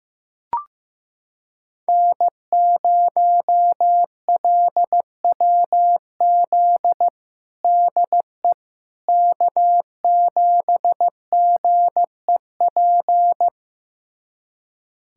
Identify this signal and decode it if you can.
Morse code, heard as a keyed tone, about 15 wpm, 700 Hz. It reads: N0LWZ DE K7GEP
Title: USA Callsign DE Callsign - 5 Characters - 15wpm